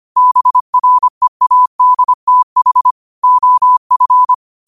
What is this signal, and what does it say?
Morse code, heard as a keyed tone, about 25 wpm, 1000 Hz. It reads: BREADTH OF
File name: qotdfiles / 25wpm